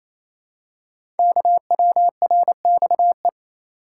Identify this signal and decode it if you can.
Morse code, heard as a keyed tone, about 29 wpm, 700 Hz. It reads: KWRXE
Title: Code Groups - 5 Letters - 28wpm